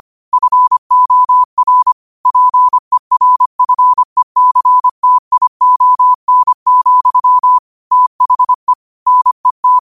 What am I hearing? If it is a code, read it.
Morse code, heard as a keyed tone, about 25 wpm, 1000 Hz. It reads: FOR PERFECTION, THE NET